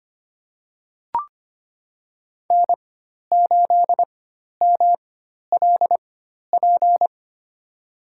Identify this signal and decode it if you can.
Morse code, heard as a keyed tone, about 25 wpm, spaced as for 12 wpm, 700 Hz. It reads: N8MLP
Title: USA Call Signs - 5 Characters - 4x Character Spacing - 25wpm